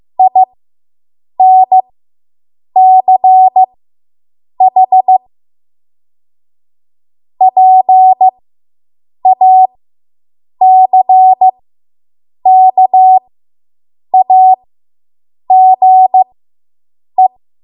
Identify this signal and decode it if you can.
Morse code, heard as a keyed tone, about 15 wpm, spaced as for 7 wpm, 750 Hz. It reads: INCH PACKAGE